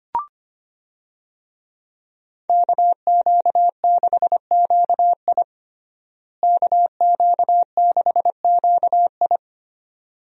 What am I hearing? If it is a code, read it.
Morse code, heard as a keyed tone, about 25 wpm, 700 Hz. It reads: KQ6QI KQ6QI
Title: USA Call Signs - 5 Characters - Repeated 2x - 25wpm